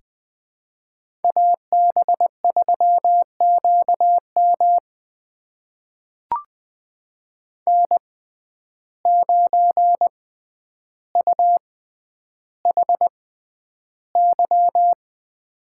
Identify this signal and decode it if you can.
Morse code, heard as a keyed tone, about 20 wpm, spaced as for 10 wpm, 700 Hz. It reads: AB3QM N9UHY
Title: USA Call Signs - 5 Characters - 6x Character Spacing - 20wpm